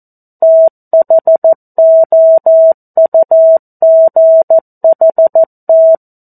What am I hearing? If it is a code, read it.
Morse code, heard as a keyed tone, about 14 wpm, 650 Hz. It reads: THOUGHT
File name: 02-Chapter-02 14 WPM